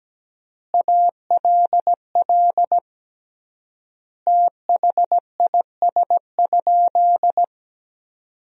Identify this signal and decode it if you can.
Morse code, heard as a keyed tone, about 17 wpm, 700 Hz. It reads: ALL THIS?
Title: 3x Word-Spacing - Sentences from Top 100 Words - 17wpm